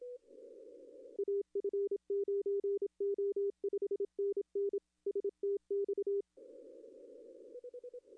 Morse code, heard as a keyed tone, about 27 wpm, 400 Hz. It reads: AF9O 5NN STX
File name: contests / arrl-160-k5tr-op-w5tn -1300